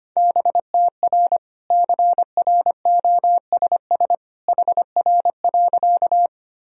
Morse code, heard as a keyed tone, about 25 wpm, 700 Hz. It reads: BTR CROSS HR.
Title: Sentences from Top 400 Words and Abbreviations 25wpm